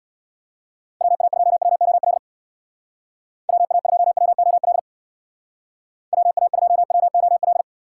Morse code, heard as a keyed tone, about 63 wpm, 700 Hz. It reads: WA2WOL WA2WOL WA2WOL